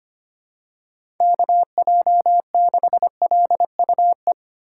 Morse code, heard as a keyed tone, about 25 wpm, 700 Hz. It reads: KJ6LUE
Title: USA Call Signs - 6 Characters - 6x Character Spacing - 25wpm